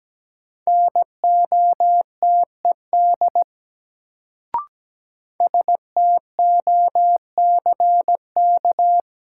Morse code, heard as a keyed tone, about 17 wpm, 700 Hz. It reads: NOTED STOCK
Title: Sets of 5 Words - New Word Drill - 17wpm